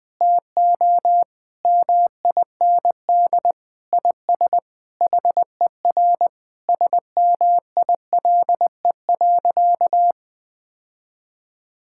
Morse code, heard as a keyed tone, about 20 wpm, 700 Hz. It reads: TO MIND IS HER SMILE.
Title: Sentences with 8-Word Phrases - 20wpm